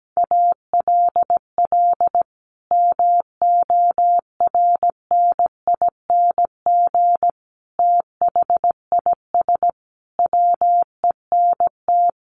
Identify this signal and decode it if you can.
Morse code, heard as a keyed tone, about 17 wpm, 700 Hz. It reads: ALL MORNING THIS WENT
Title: Sentences from Top 1500 Words 17wpm